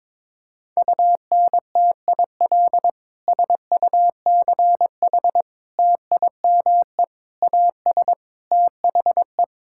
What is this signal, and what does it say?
Morse code, heard as a keyed tone, about 22 wpm, 700 Hz. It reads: UNTIL SUCH TIME AS THE
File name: Sets of 9 Words - 22wpm